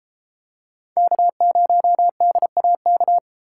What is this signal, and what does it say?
Morse code, heard as a keyed tone, about 30 wpm, 700 Hz. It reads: K0DAK